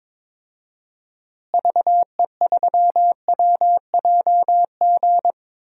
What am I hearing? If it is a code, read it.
Morse code, heard as a keyed tone, about 21 wpm, 700 Hz. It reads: VE3WJG
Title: DXCC-100 Call Signs - 6 Characters - Repeated 2x - 22wpm